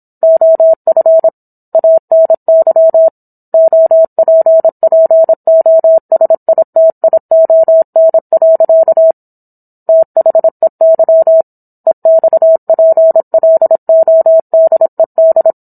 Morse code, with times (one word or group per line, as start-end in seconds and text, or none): 0.2-1.3 OF
1.8-3.1 ANY
3.5-9.1 OPPOSITION.
9.9-11.4 THEY
11.9-15.5 EXPLODED